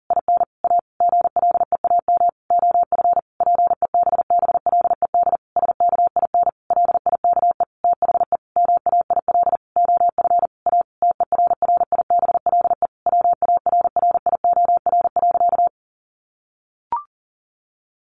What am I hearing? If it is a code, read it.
Morse code, heard as a keyed tone, about 40 wpm, 700 Hz. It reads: IN A GLEAM OF PEBBLED SKIN LIKE THE MAIL OF A TERRIBLE WARRIOR.
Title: A Sound Of Thunder 40wpm